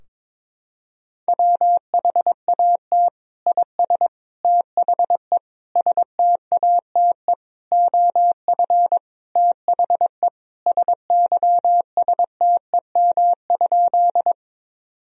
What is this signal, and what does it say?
Morse code, heard as a keyed tone, about 22 wpm, 700 Hz. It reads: WHAT IS THE STATE OF THE SYSTEM?
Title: Sentences from Top 1000 Words - Review Part 3 - 22wpm